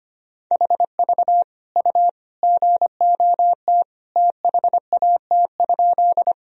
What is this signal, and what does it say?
Morse code, heard as a keyed tone, about 25 wpm, 700 Hz. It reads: HV U GOT THAT?
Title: Sentences from Top 300 Words and Abbreviations 25wpm